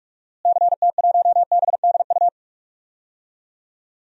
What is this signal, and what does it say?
Morse code, heard as a keyed tone, about 45 wpm, 700 Hz. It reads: CT1BDU
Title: DXCC-100 Call Signs - 6 Characters - Repeated 2x - 45wpm